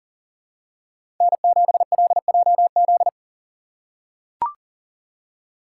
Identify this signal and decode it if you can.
Morse code, heard as a keyed tone, about 40 wpm, 700 Hz. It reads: N7LJZ
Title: USA Call Signs - 5 Characters - 6x Character Spacing - 40wpm